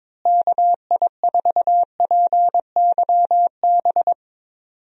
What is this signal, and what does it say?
Morse code, heard as a keyed tone, about 21 wpm, 700 Hz. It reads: KI4PYB
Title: USA Call Signs - 6 Characters - Repeated 3x - 22wpm